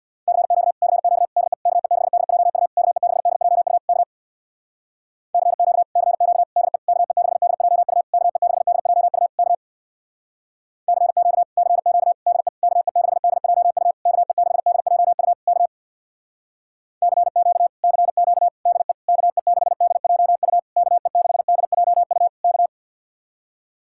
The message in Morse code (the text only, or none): CQ CQ DE KE6DJU KE6DJU K CQ CQ DE KE6DJU KE6DJU K CQ CQ DE KE6DJU KE6DJU K CQ CQ DE KE6DJU KE6DJU K